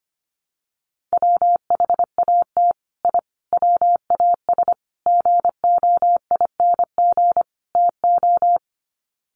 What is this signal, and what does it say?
Morse code, heard as a keyed tone, about 25 wpm, 700 Hz. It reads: WHAT I WAS GOING TO
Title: Sets of 5 Words Encore2 25wpm